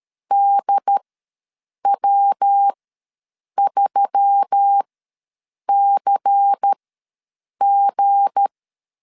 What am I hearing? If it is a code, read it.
Morse code, heard as a keyed tone, about 13 wpm, spaced as for 7 wpm, 800 Hz. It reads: DW3CG
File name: LEZIONI / 7 GF9Z / 7 G ws 12wpm 3140